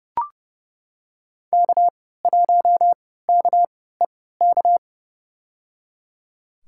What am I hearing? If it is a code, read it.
Morse code, heard as a keyed tone, about 29 wpm, spaced as for 17 wpm, 700 Hz. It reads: K1KEK
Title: USA Call Signs - 5 Characters - 3x Character Spacing - 30wpm